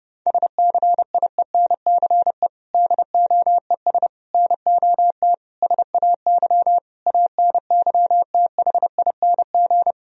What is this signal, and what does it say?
Morse code, heard as a keyed tone, about 32 wpm, 700 Hz. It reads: SCIENCE DOES NOT SAY ANYTHING
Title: Sentences from Top 1000 Words - Review Part 2 - 30wpm